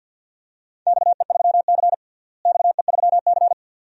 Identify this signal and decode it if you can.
Morse code, heard as a keyed tone, about 50 wpm, 700 Hz. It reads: XE3C XE3C
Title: DXCC-100 Call Signs - 4 Characters - Repeated 2x - 50wpm